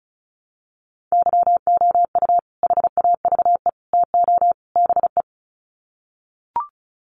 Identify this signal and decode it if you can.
Morse code, heard as a keyed tone, about 34 wpm, 700 Hz. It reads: YOU HAVE TO BE
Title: Sets of 4 Words 35wpm